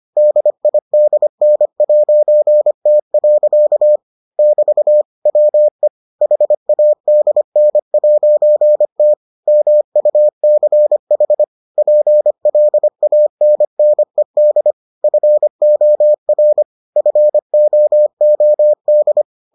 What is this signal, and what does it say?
Morse code, heard as a keyed tone, about 25 wpm, 600 Hz. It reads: DIDN'T. <BT> WE HADN'T MUCH PLANNED FOR FOOD